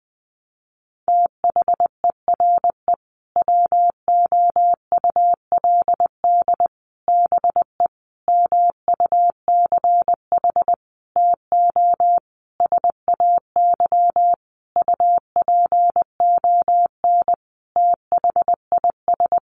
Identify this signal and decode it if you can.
Morse code, heard as a keyed tone, about 20 wpm, 700 Hz. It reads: THERE WOULD BE MUCH TO SAY UPON THIS
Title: Sentences from Top 1200 Words 20wpm